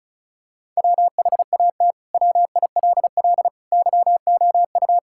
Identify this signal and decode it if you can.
Morse code, heard as a keyed tone, about 34 wpm, 700 Hz. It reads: WHAT WILL YOU
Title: Sentences from Top 1000 Words - Review Part 6 - 35wpm